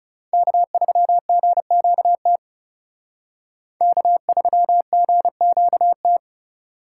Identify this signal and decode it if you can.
Morse code, heard as a keyed tone, about 37 wpm, 700 Hz. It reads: K3GQT K3GQT
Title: Speed Racing - USA 1x3 Call Signs 30wpm